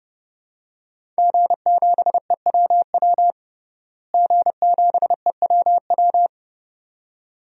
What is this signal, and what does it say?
Morse code, heard as a keyed tone, about 32 wpm, 700 Hz. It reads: G7EWW G7EWW
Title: DXCC-100 Call Signs - 5 Characters - Repeated 2x - 30wpm